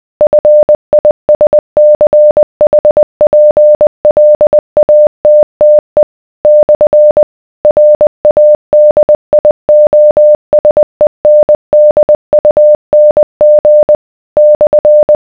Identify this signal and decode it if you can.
Morse code, heard as a keyed tone, about 20 wpm, 600 Hz. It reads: FISCHPLATTE / RADIOSENDUNG /